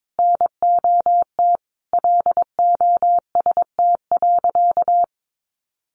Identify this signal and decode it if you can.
Morse code, heard as a keyed tone, about 22 wpm, 700 Hz. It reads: NOT LOST.
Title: Sentences from Top 1400 Words 22wpm